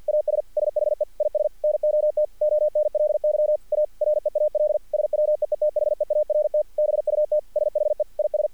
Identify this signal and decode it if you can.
Morse code, heard as a keyed tone, about 46 wpm, 600 Hz. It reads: AS SHE IS NOT ONLY A REAL SWEETHEART BUT SHE IS
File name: archived / CRSnet